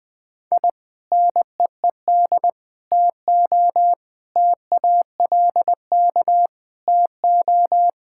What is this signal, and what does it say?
Morse code, heard as a keyed tone, about 19 wpm, 700 Hz. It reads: I NEED TO TALK TO
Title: Sentences with 4-Word Phrases - 20wpm